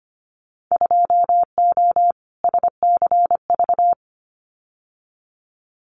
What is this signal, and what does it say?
Morse code, heard as a keyed tone, about 25 wpm, 700 Hz. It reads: 2O SCV
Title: Winter Field Day Exchanges Encore 25wpm